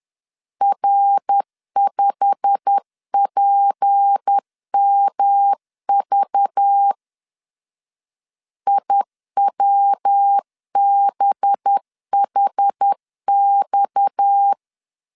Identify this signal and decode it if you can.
Morse code, heard as a keyed tone, about 10 wpm, 800 Hz. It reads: R5PMV IWBHX
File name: CW / DET / 1 9 1500C S A 10wpm 4511